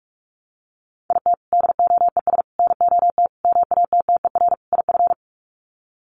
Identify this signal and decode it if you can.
Morse code, heard as a keyed tone, about 44 wpm, 700 Hz. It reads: IT DOES NOT MATTER IF
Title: Sets of 5 Words Encore 45wpm